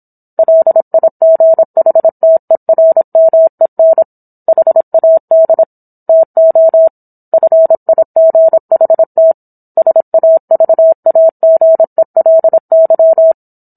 Morse code, heard as a keyed tone, about 27 wpm, 650 Hz. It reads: LIGHTERMEN HAD TO FIGHT SAVAGELY
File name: Book One / 17-Chapter 17 - 26 WPM